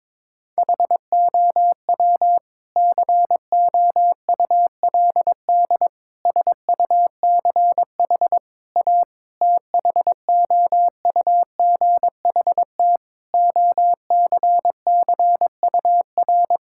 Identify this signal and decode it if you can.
Morse code, heard as a keyed tone, about 22 wpm, 700 Hz. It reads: HOW COULD SUCH A THOUGHT OCCUR
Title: Sentences from Top 1000 Words - Review Part 6 - 22wpm